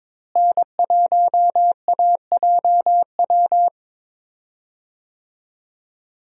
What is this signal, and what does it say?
Morse code, heard as a keyed tone, about 22 wpm, 700 Hz. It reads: N1AJW
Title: USA Call Signs - 5 Characters - Repeated 2x - 22wpm